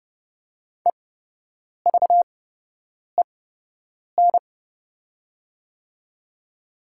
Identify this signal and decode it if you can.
Morse code, heard as a keyed tone, about 32 wpm, spaced as for 8 wpm, 700 Hz. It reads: EVEN
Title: 8x Character Spacing - Top 300 Words - 30wpm